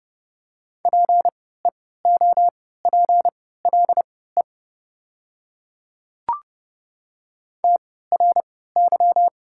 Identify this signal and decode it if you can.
Morse code, heard as a keyed tone, about 30 wpm, spaced as for 17 wpm, 700 Hz. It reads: PEOPLE TRY
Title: 3x Character Spacing - Top 200 Words - 30wpm